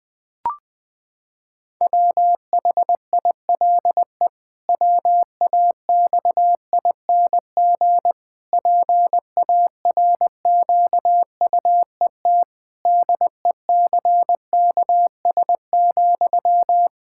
Morse code, heard as a keyed tone, about 21 wpm, 700 Hz. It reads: WHILE WAXING PARQUET DECKS,
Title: Pangram Challenge v2 - 20wpm